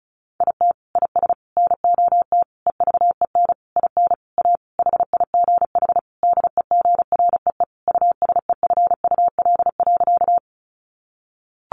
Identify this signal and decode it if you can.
Morse code, heard as a keyed tone, about 34 wpm, 700 Hz. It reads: IT IS NOT EVEN IN A HIGH DEGREE USEFUL.
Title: Sentences from Top 1500 Words 35wpm